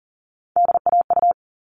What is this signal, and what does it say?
Morse code, heard as a keyed tone, about 40 wpm, 700 Hz. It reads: DAU
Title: All Cut Numbers - 3-Digit 40wpm